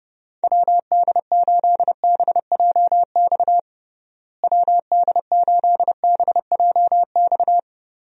Morse code, heard as a keyed tone, about 30 wpm, 700 Hz. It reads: WD8BJX WD8BJX
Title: USA Call Signs - 6 Characters - Repeated 3x - 30wpm